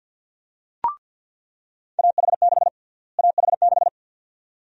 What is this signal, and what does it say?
Morse code, heard as a keyed tone, about 47 wpm, 700 Hz. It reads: AH6 AH6
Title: USA Call Sign Prefixes - Repeated 2x - 50wpm